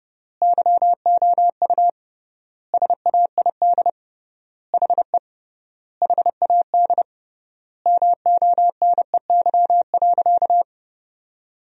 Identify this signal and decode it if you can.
Morse code, heard as a keyed tone, about 30 wpm, 700 Hz. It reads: YOU SAID HE HAD MONEY.
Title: 3x Word-Spacing - Sentences from Top 500 Words - 30wpm